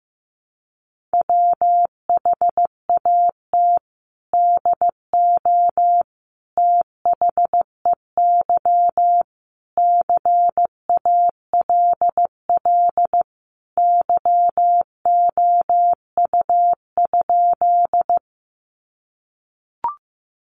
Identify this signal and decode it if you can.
Morse code, heard as a keyed tone, about 15 wpm, 700 Hz. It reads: WHAT DO THEY CALL YOU?
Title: Sentences from Top 100 Words 15wpm